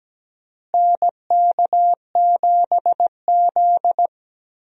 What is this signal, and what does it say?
Morse code, heard as a keyed tone, about 17 wpm, 700 Hz. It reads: NK7Z
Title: USA Call Signs - 4 Characters - Repeated 3x - 17wpm